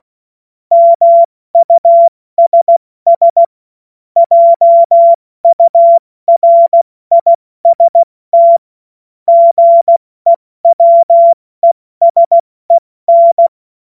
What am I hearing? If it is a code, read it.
Morse code, heard as a keyed tone, about 16 wpm, 680 Hz. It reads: MUSS JURIST GEWESEN
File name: Morsekurs / Witze / Witz 16. August 70 F80